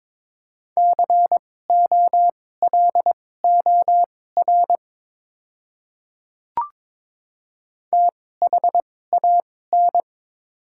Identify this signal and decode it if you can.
Morse code, heard as a keyed tone, about 22 wpm, spaced as for 16 wpm, 700 Hz. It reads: COLOR THAN
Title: 2x Character Spacing - Top 300 Words - 22wpm